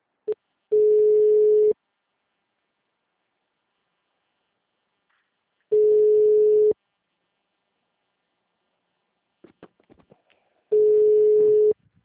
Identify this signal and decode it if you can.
Ringback tone (Europe, 425 Hz).